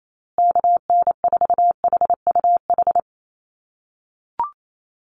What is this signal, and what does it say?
Morse code, heard as a keyed tone, about 28 wpm, 700 Hz. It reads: KN4HUH
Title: USA Call Signs Repeated 3x 28wpm